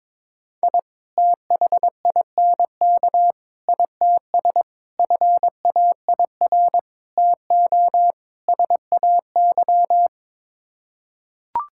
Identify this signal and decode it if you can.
Morse code, heard as a keyed tone, about 22 wpm, 700 Hz. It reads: I THINK ITS FAIR TO SAY
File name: practice / 22wpm / Sets of 6 Words - 22wpm